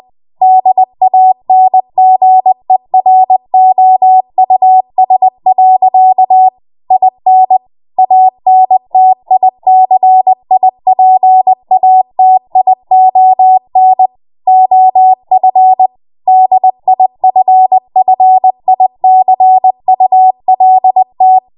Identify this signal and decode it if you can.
Morse code, heard as a keyed tone, about 20 wpm, 750 Hz. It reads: DANGEROUS. IN ANTICIPATION OF DIFFICULT
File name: Morse / 20WPM